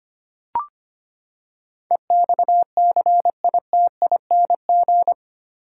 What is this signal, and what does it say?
Morse code, heard as a keyed tone, about 25 wpm, 700 Hz. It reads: EXCITING
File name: Eight Letter Words - 25wpm